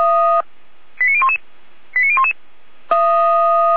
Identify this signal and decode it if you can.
Five-tone selective call (ZVEI), heard as quick runs of five tones, tones of about 70 ms. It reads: 89011 89011